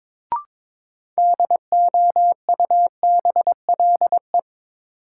Morse code, heard as a keyed tone, about 22 wpm, 700 Hz. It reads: DOUBLE
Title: Sentences from Top 1400 Words - New Word Drill - 22wpm